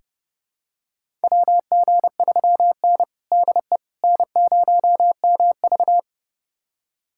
Morse code, heard as a keyed tone, about 30 wpm, 700 Hz. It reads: WG3N DE N0MV
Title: USA Callsign DE Callsign - 4 Characters - 30wpm